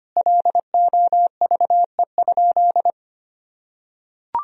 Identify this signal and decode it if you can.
Morse code, heard as a keyed tone, about 25 wpm, 700 Hz. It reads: LOVE?
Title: Sentences from Top 1000 Words - Review Part 2 - 25wpm